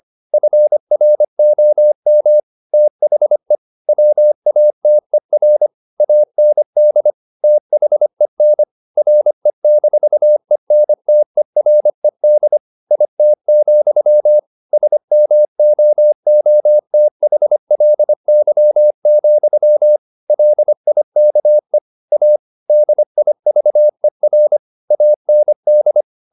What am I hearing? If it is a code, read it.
Morse code, heard as a keyed tone, about 25 wpm, 600 Hz. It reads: FROM THE WATER AND THEN RE-ENTERED IT, SMOOTHLY, LIKE A DIVER AND
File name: the old man and the sea part 1 25wpm